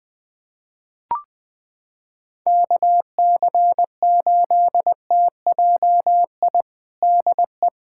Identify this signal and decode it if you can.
Morse code, heard as a keyed tone, about 21 wpm, 700 Hz. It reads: KC8TJI DE
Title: USA Callsign DE Callsign - 6 Characters - 20wpm